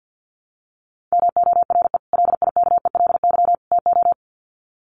Morse code, heard as a keyed tone, about 48 wpm, 700 Hz. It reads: MORE LIKELY TO
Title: Sets of 3 Words 50wpm